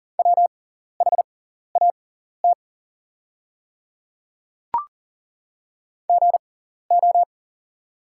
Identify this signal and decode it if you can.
Morse code, heard as a keyed tone, about 42 wpm, spaced as for 14 wpm, 690 Hz. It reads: WHAT GO
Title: 6x Character Spacing - Top 100 Words - 40wpm